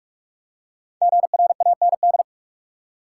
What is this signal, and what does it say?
Morse code, heard as a keyed tone, about 46 wpm, 700 Hz. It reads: GRAND
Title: Five Letter Words - 45wpm